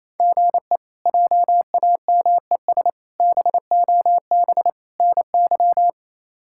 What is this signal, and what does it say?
Morse code, heard as a keyed tone, about 27 wpm, 700 Hz. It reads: GE JAMES BOB NY
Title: K1USN SST QSO Practice - 28wpm